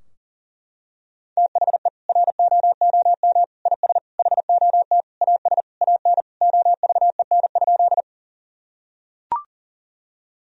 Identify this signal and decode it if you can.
Morse code, heard as a keyed tone, about 40 wpm, 700 Hz. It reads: THE ROOM IS HOT AS AN OVEN?
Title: Sentences from Top 1000 Words - Review Part 4 - 40wpm